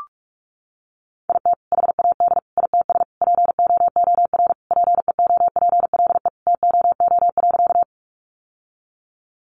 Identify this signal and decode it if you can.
Morse code, heard as a keyed tone, about 44 wpm, 700 Hz. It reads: IT HAD ITS POOR PEOPLE TOO.
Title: Sentences from Top 1000 Words 45wpm